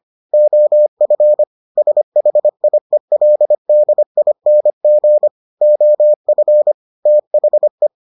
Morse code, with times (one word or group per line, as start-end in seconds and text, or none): 0.3-1.4 OF
1.8-5.3 SHIELDING
5.6-6.7 OF
7.1-7.9 THE